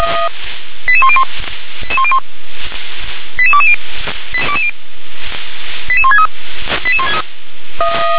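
Five-tone selective call (ZVEI), heard as quick runs of five tones, tones of about 70 ms. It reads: 88101 88220 88163
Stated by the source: ZVEI five-tone selective call, received on an SDR.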